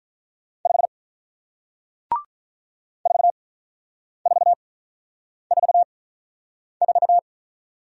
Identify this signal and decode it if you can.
Morse code, heard as a keyed tone, about 47 wpm, spaced as for 9 wpm, 700 Hz. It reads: 5 4444